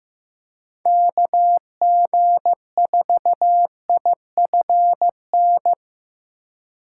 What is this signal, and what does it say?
Morse code, heard as a keyed tone, about 15 wpm, 700 Hz. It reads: KG4IFN
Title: USA Call Signs - 4x Character Spacing - 15wpm